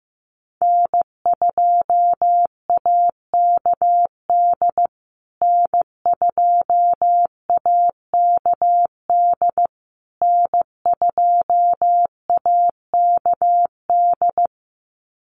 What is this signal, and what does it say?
Morse code, heard as a keyed tone, about 15 wpm, 700 Hz. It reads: N2AKD N2AKD N2AKD
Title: USA Call Signs Repeated 3x 15wpm